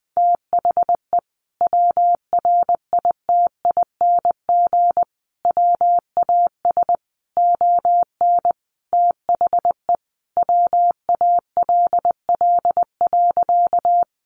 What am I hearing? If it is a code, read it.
Morse code, heard as a keyed tone, about 20 wpm, 700 Hz. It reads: THE WRITING WAS ON THE WALL.
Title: Sentences from Top 1100 Words 20wpm